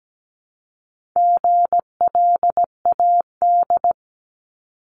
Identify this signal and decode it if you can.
Morse code, heard as a keyed tone, about 17 wpm, 700 Hz. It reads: GLAD
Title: Multi-letter Phonemes 17wpm